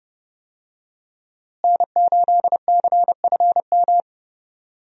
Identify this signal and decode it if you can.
Morse code, heard as a keyed tone, about 30 wpm, 700 Hz. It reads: N8CFM
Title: USA Call Signs - 5 Characters - Repeated 3x - 30wpm